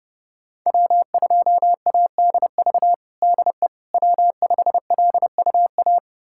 Morse code, heard as a keyed tone, about 30 wpm, 700 Hz. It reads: W2ADV DE W5LUA
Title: USA Callsign DE Callsign - 5 Characters - 30wpm